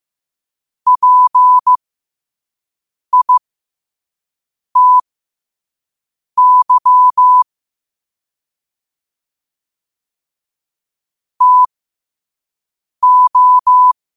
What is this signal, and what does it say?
Morse code, heard as a keyed tone, about 15 wpm, spaced as for 5 wpm, 1000 Hz. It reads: PITY TO
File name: qotdfiles / 5wpm